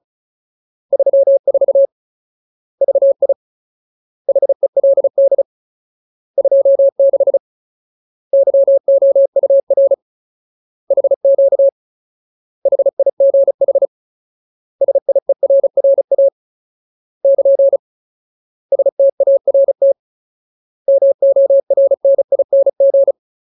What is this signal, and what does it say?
Morse code, heard as a keyed tone, about 34 wpm, 550 Hz. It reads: 34 VI HELD 26 YOUR HQ HIGH SIERRA <KN> START MORNING
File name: Ultimate Word List / 35WPM Ultimate